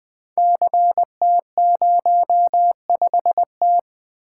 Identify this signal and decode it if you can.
Morse code, heard as a keyed tone, about 20 wpm, 700 Hz. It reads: CT05T
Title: Code Groups - 5 Letter:Numbers - 20wpm